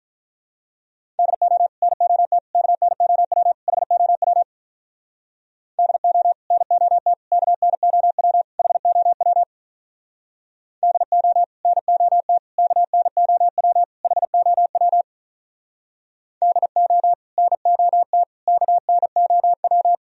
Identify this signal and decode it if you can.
Morse code, heard as a keyed tone, about 45 wpm, 700 Hz. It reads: DO NOT KNOW HOW DO NOT KNOW HOW DO NOT KNOW HOW DO NOT KNOW